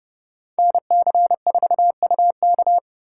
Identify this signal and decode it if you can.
Morse code, heard as a keyed tone, about 35 wpm, 700 Hz. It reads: NC4UK
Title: Speed Racing - USA 2x2 Call Signs 30wpm